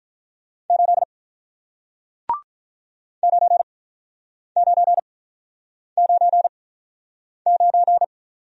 Morse code, heard as a keyed tone, about 50 wpm, spaced as for 9 wpm, 700 Hz. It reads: 8 9999